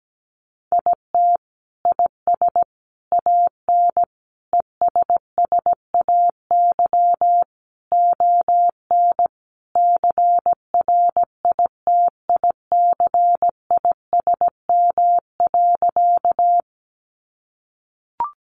Morse code, heard as a keyed tone, about 17 wpm, 700 Hz. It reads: IT IS AN ESSAY ON CRITICISM.
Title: Sentences from Top 1500 Words 17wpm